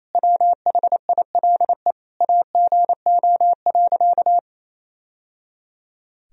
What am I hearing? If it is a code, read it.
Morse code, heard as a keyed tone, about 28 wpm, 700 Hz. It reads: WHILE AGO.
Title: Sentences from Top 1000 Words - Review Part 3 - 28wpm